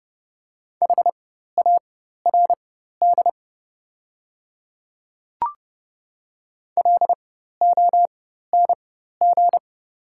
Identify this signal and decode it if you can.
Morse code, heard as a keyed tone, about 30 wpm, spaced as for 14 wpm, 700 Hz. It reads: HARD LONG
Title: 4x Character Spacing - Top 300 Words - 30wpm